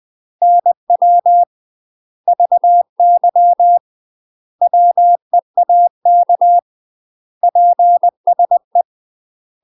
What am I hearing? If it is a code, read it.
Morse code, heard as a keyed tone, about 20 wpm, 700 Hz. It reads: NW VY WEAK PSE
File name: UK MASTER QSOs TRAINING / 20WPM / UK MASTER QSOs 0040